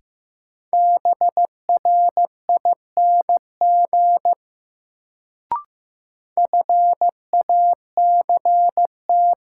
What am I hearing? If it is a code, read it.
Morse code, heard as a keyed tone, about 15 wpm, 700 Hz. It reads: BRING FACT
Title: Sentences from Top 500 Words - New Word Drill - 15wpm